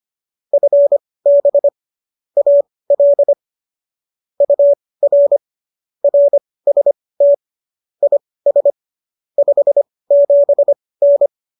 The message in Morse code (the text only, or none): FB AL UR RST IS 57N